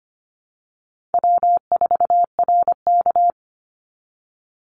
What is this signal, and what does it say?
Morse code, heard as a keyed tone, about 25 wpm, 700 Hz. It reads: W4RK
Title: Easy Call Signs - 25wpm